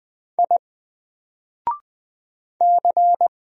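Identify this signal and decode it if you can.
Morse code, heard as a keyed tone, about 20 wpm, 700 Hz. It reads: I C
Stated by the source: Single letter and number Morse drill at 20 wpm, from N4LCD.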